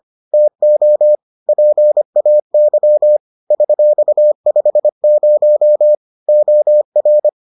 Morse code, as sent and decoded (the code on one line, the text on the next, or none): - --- / .--. .- -.-- / ...-..- ..... ----- / --- .-.
TO PAY $50 OR